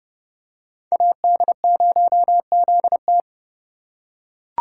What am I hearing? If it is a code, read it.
Morse code, heard as a keyed tone, about 32 wpm, 700 Hz. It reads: AD0ZT
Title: USA Call Signs - 5 Characters - 2x Character Spacing - 30wpm